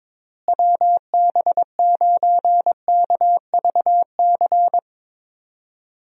Morse code, heard as a keyed tone, about 22 wpm, 700 Hz. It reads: WB9KVC
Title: USA Call Signs - 6 Characters - Repeated 3x - 22wpm